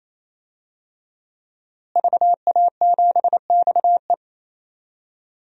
Morse code, heard as a keyed tone, about 27 wpm, 700 Hz. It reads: VA7XE
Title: DXCC-100 Call Signs - 5 Characters - Repeated 3x - 28wpm